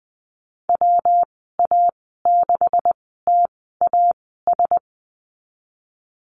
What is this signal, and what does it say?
Morse code, heard as a keyed tone, about 20 wpm, spaced as for 15 wpm, 700 Hz. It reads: WA6TAS
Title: 140 Lesson 23g - Letter f - Call Signs 20wpm